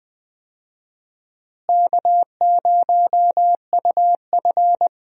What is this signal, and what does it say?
Morse code, heard as a keyed tone, about 20 wpm, 700 Hz. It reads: K0UF
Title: USA Call Signs - 4 Characters - Repeated 3x - 20wpm